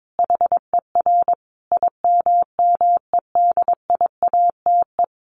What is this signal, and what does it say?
Morse code, heard as a keyed tone, about 22 wpm, 700 Hz. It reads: HER IMMEDIATE
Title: Sentences from Top 1200 Words 22wpm